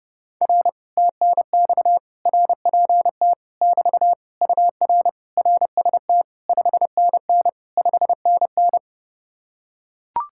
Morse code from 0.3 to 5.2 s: R TNX RPT <BT> UR

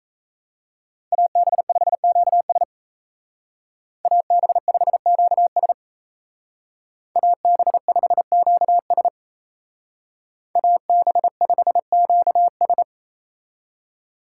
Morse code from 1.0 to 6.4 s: AB5QS AB5QS

